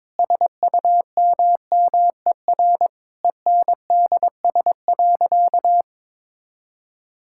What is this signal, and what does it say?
Morse code, heard as a keyed tone, about 22 wpm, 700 Hz. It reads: SUMMER ENDS.
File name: Difficult Sentences with Hints - 22wpm